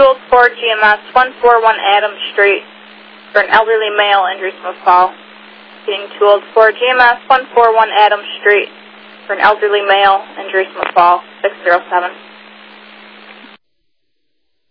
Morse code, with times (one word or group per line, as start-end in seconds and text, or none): none